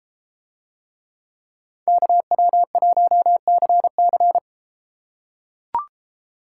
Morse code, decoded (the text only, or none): KW1CC